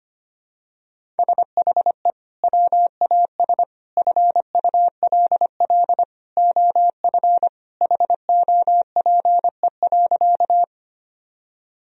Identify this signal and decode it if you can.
Morse code, heard as a keyed tone, about 25 wpm, 700 Hz. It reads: SHE WAS FULL OF HOPE.
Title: Sentences from Top 1000 Words - Review Part 3 - 25wpm